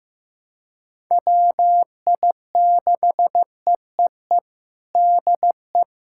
Morse code, heard as a keyed tone, about 15 wpm, 700 Hz. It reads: WI6EEE DE
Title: USA Callsign DE Callsign - 6 Characters - 15wpm